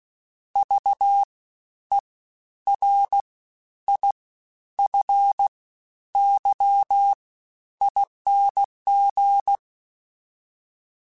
Morse code, heard as a keyed tone, about 16 wpm, spaced as for 10 wpm, 800 Hz. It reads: VERIFYING
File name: Special INGs-mp3s / Special INGs-mp3s / SlowToFaster ing @15